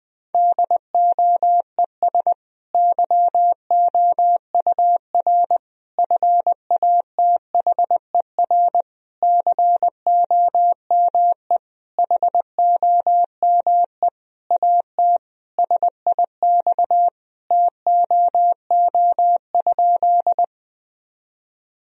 Morse code, heard as a keyed tone, about 20 wpm, 700 Hz. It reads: DOES YOUR FATHER COME HOME AT SIX TOO?
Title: Sentences from Top 1000 Words - Review Part 2 - 20wpm